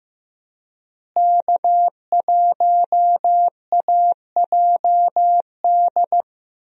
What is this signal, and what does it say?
Morse code, heard as a keyed tone, about 15 wpm, 700 Hz. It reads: K1AJD
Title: USA Call Signs - 5 Characters - Repeated 2x - 15wpm